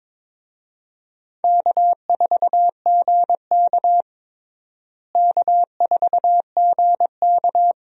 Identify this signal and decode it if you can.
Morse code, heard as a keyed tone, about 22 wpm, 700 Hz. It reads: K4GK K4GK
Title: USA Call Signs - 4 Characters - Repeated 3x - 22wpm